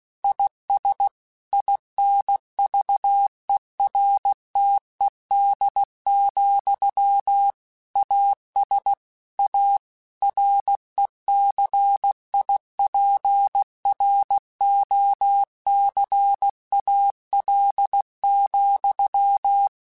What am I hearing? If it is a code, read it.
Morse code, heard as a keyed tone, about 16 wpm, 800 Hz. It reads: IS INVERTED, AS A RECIPROCAL,
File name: Technical Subjects / Technical Subjects-mp3s / XC @15